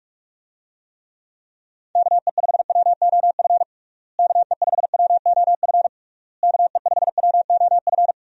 Morse code, heard as a keyed tone, about 46 wpm, 700 Hz. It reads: KE5WOF KE5WOF KE5WOF
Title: USA Call Signs - 6 Characters - Repeated 3x - 45wpm